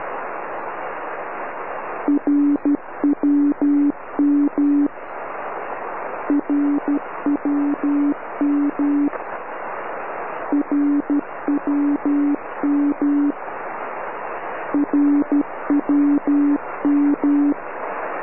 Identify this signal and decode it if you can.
Morse code, heard as a keyed tone, about 12 wpm, 290 Hz. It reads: RWM RWM RWM RWM
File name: RWM cut